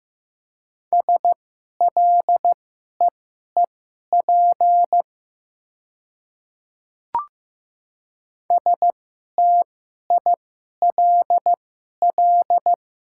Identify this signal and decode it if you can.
Morse code, heard as a keyed tone, about 15 wpm, spaced as for 11 wpm, 700 Hz. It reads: SLEEP STILL
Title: 2x Character Spacing - Top 500 Words - 15wpm